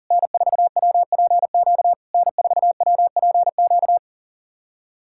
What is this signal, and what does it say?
Morse code, heard as a keyed tone, about 40 wpm, 700 Hz. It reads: N4WPQ N4WPQ
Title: SOTA QSO Practice 40wpm